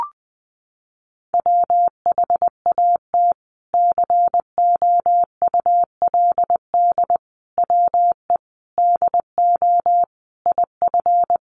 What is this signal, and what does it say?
Morse code, heard as a keyed tone, about 20 wpm, 700 Hz. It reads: WHAT COULD WE DO IF
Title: Sentences from Top 400 Words 20wpm